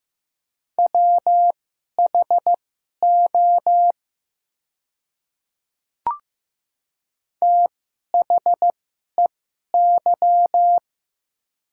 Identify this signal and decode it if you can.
Morse code, heard as a keyed tone, about 15 wpm, spaced as for 11 wpm, 700 Hz. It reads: WHO THEY
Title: 2x Character Spacing - Top 100 Words - 15wpm